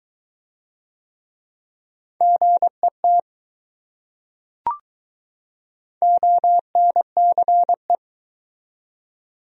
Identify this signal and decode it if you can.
Morse code, heard as a keyed tone, about 21 wpm, 700 Hz. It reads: GET ONCE